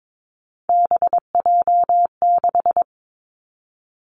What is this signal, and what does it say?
Morse code, heard as a keyed tone, about 22 wpm, 700 Hz. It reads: BJ6
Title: Three Challenging Letter-Number Combinations 22wpm